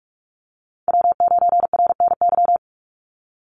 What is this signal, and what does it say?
Morse code, heard as a keyed tone, about 46 wpm, 700 Hz. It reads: W9RNY
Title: Call Signs - 45wpm